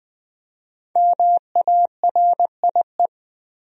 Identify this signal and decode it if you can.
Morse code, heard as a keyed tone, about 20 wpm, 700 Hz. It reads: MARIE
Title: USA Common Names - 20wpm